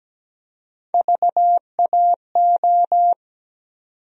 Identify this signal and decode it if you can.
Morse code, heard as a keyed tone, about 17 wpm, 700 Hz. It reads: VAO
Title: USA Call Sign Postfixes - 3 Letters 17wpm